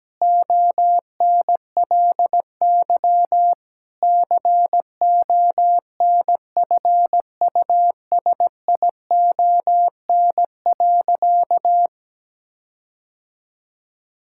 Morse code, heard as a keyed tone, about 17 wpm, 700 Hz. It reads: ONLY CONFUSION.
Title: Sentences from Top 1600 Words 17wpm